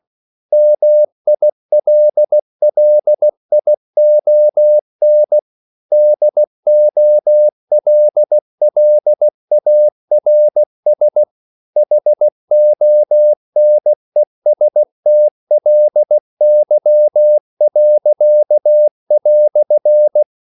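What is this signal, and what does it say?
Morse code, heard as a keyed tone, about 16 wpm, 600 Hz. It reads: MILLION DOLLARS HONESTLY."
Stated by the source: English quotations sent in CW as practice at 16 words per minute.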